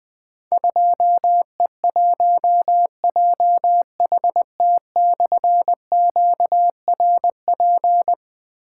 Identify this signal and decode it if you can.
Morse code, heard as a keyed tone, about 21 wpm, 700 Hz. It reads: 2E1JHT/QRP
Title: Difficult Call Signs 20wpm